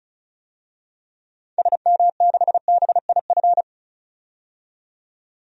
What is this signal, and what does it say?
Morse code, heard as a keyed tone, about 36 wpm, 700 Hz. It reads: SM6BIF
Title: DXCC-100 Call Signs - 6 Characters - 35wpm